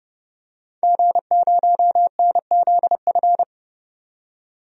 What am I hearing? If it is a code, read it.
Morse code, heard as a keyed tone, about 29 wpm, 700 Hz. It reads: G0NZF
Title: DXCC-100 Call Signs - 5 Characters - 30wpm